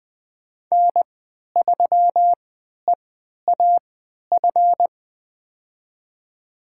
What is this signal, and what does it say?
Morse code, heard as a keyed tone, about 20 wpm, spaced as for 11 wpm, 700 Hz. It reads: N3EAF
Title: USA Call Signs - 5 Characters - 3x Character Spacing - 20wpm